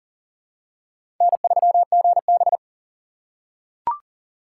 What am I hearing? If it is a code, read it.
Morse code, heard as a keyed tone, about 40 wpm, 700 Hz. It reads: N3GB
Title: USA Call Signs - 4 Characters - 3x Character Spacing - 40wpm